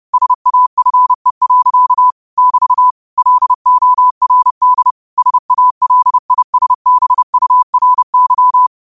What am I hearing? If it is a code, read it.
Morse code, heard as a keyed tone, about 30 wpm, 1000 Hz. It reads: SAFE. <BT> LORD SALISBURY
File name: qotdfiles / 30wpm